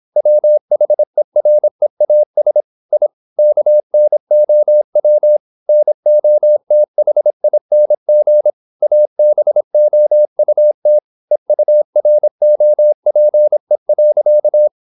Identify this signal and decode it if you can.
Morse code, heard as a keyed tone, about 26 wpm, 600 Hz. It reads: WHEREAS I KNOW NOTHING ABOUT EUROPE.
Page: Never 26wpm | CW med Gnister